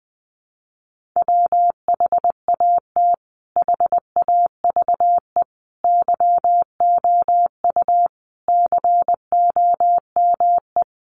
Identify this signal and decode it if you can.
Morse code, heard as a keyed tone, about 20 wpm, 700 Hz. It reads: WHAT HAVE YOU COME
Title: Sentences from Top 100 Words 20wpm